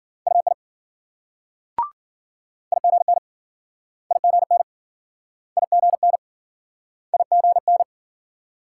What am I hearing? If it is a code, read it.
Morse code, heard as a keyed tone, about 53 wpm, 700 Hz. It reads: UI IGN IGN IGN IGN